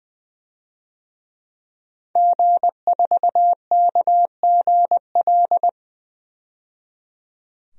Morse code, heard as a keyed tone, about 20 wpm, 700 Hz. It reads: G4KGL
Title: DXCC-100 Call Signs - 5 Characters - 20wpm